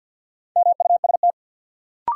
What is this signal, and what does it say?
Morse code, heard as a keyed tone, about 47 wpm, 700 Hz. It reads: MUST
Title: Sentences from Top 200 Words - New Word Drill - 50wpm